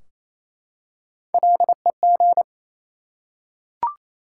Morse code, heard as a keyed tone, about 28 wpm, 700 Hz. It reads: LEG
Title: Three Letter Words - 28wpm